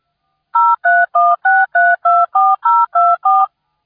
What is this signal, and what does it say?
DTMF, heard as telephone touch tones, tones of about 220 ms. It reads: 0316324024